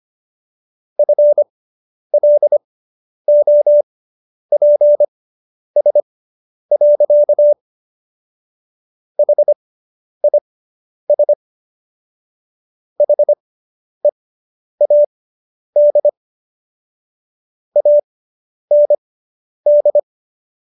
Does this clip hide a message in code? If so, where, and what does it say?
Yes: Morse code from 1.0 to 20.0 s, heard as a keyed tone, about 25 wpm, spaced as for 10 wpm, 590 Hz. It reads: FLOPS. HIS HEAD AND